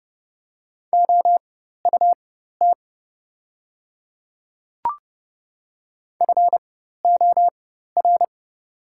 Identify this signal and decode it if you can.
Morse code, heard as a keyed tone, about 29 wpm, spaced as for 14 wpm, 700 Hz. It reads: OUT FOR
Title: 4x Character Spacing - Top 300 Words - 30wpm